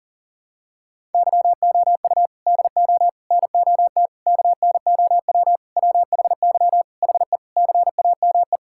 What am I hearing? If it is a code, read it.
Morse code, heard as a keyed tone, about 40 wpm, 700 Hz. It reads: YOU DO NOT KNOW WHY HE CAME
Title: Sentences from Top 1000 Words - Review Part 6 - 40wpm